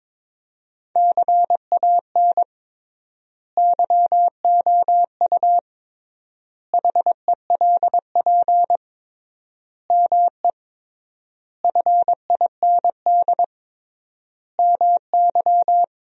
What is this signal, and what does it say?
Morse code, heard as a keyed tone, about 21 wpm, 700 Hz. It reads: CAN YOU HELP ME FIND MY
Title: 3x Word-Spacing - Sentences from Top 200 Words - 22wpm